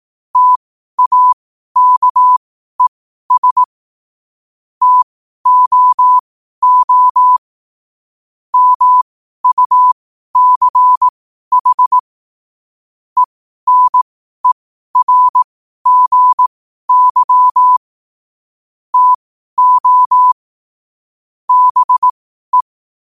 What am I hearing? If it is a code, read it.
Morse code, heard as a keyed tone, about 18 wpm, spaced as for 12 wpm, 1000 Hz. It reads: TAKES TOO MUCH ENERGY TO BE